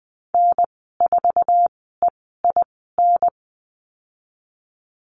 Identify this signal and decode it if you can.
Morse code, heard as a keyed tone, about 20 wpm, spaced as for 14 wpm, 700 Hz. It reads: N4EIN
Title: 082 Lesson 14g - Number 2 - Call Signs 20wpm